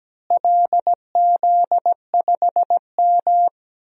Morse code, heard as a keyed tone, about 17 wpm, 700 Hz. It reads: LZ5M
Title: DXCC-100 Call Signs - 4 Characters - Repeated 3x - 17wpm